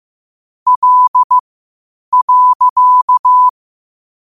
Morse code, heard as a keyed tone, about 15 wpm, spaced as for 8 wpm, 1000 Hz. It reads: L.
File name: qotdfiles / 8wpm